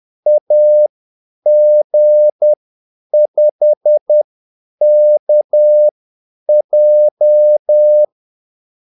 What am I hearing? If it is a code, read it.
Morse code, heard as a keyed tone, about 10 wpm, spaced as for 8 wpm, 600 Hz. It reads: AG5KJ
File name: Callsigns-10-8wpm